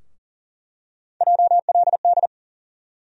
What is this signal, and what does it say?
Morse code, heard as a keyed tone, about 40 wpm, 700 Hz. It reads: JLD
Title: USA Call Sign Postfixes - 3 Letters 40wpm